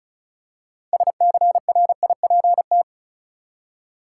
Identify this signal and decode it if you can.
Morse code, heard as a keyed tone, about 36 wpm, 700 Hz. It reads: SCRIPT
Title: Six Letter Words - 35wpm